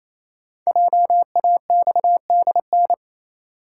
Morse code, heard as a keyed tone, about 29 wpm, 700 Hz. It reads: JAXDN
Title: Code Groups - 5 Letters - 28wpm